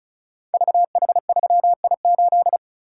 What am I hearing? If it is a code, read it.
Morse code, heard as a keyed tone, about 36 wpm, 700 Hz. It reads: VH3I8
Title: Code Groups - 5 Letter:Numbers - 35wpm